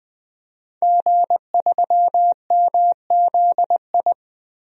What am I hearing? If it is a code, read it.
Morse code, heard as a keyed tone, about 20 wpm, 700 Hz. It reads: G3MZI
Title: DXCC-100 Call Signs - 5 Characters - Repeated 3x - 20wpm